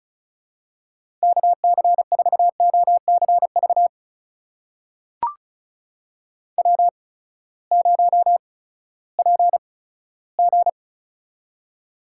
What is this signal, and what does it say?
Morse code, heard as a keyed tone, about 36 wpm, spaced as for 18 wpm, 700 Hz. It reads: KC4OCV W0PG
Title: USA Call Signs - 8x Character Spacing - 35wpm